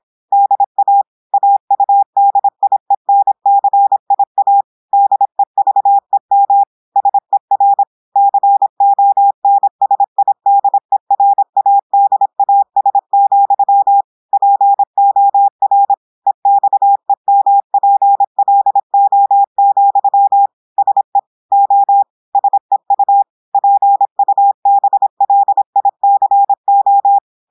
Morse code, heard as a keyed tone, about 26 wpm, 800 Hz. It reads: DA AUDIENCIA DEVEM SER CONSIDERADAS, POR EXEMPLO, SE O SEU PUBLICO
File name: portugais texte en clair 2 vitesse 26 mots minutes